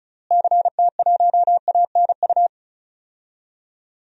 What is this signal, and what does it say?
Morse code, heard as a keyed tone, about 36 wpm, 700 Hz. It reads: CT1ANU
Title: DXCC-100 Call Signs - 6 Characters - Repeated 3x - 35wpm